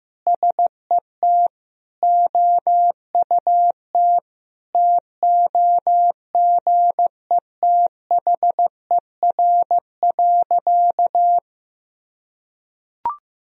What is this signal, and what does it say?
Morse code, heard as a keyed tone, about 15 wpm, 700 Hz. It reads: SET OUT TOGETHER.
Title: Sentences from Top 1000 Words - Review Part 1 - 15wpm